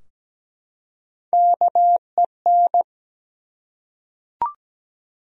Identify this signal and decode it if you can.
Morse code, heard as a keyed tone, about 17 wpm, 700 Hz. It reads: KEN
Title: USA Common Names - Max 3 Letters - 17wpm